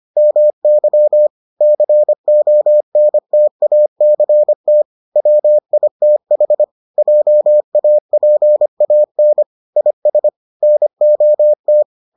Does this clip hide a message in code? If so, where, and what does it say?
Yes: Morse code from 0.2 to 11.8 s, heard as a keyed tone, about 25 wpm, 600 Hz. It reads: MY CONTACT WITH JAPAN IS NOT